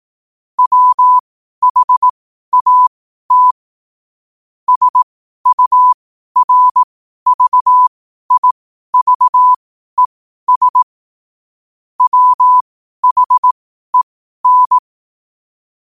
Morse code, heard as a keyed tone, about 18 wpm, spaced as for 12 wpm, 1000 Hz. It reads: WHAT SURVIVES WHEN
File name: qotdfiles / 12wpm